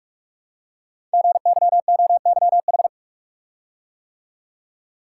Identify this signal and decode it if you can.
Morse code, heard as a keyed tone, about 46 wpm, 700 Hz. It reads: GYOYH
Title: Code Groups - 5 Letter:Numbers - 45wpm